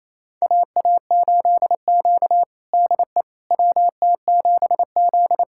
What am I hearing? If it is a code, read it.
Morse code, heard as a keyed tone, about 28 wpm, 700 Hz. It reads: AA8Q DE WT7Z
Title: USA Callsign DE Callsign - 4 Characters - 28wpm